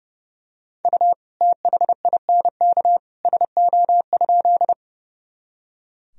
Morse code, heard as a keyed tone, about 32 wpm, 700 Hz. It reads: U THINK SO?
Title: Sentences from Top 200 Words and Abbreviations 30wpm